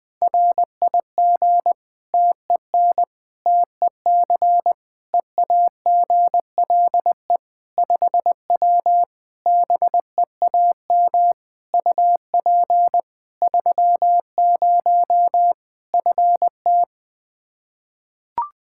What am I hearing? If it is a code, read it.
Morse code, heard as a keyed tone, about 20 wpm, 700 Hz. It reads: RIG TEN TEC EAGLE 5W BEAM UP 30 FT
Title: Practice QSO - Exchange 2 - Info - 20wpm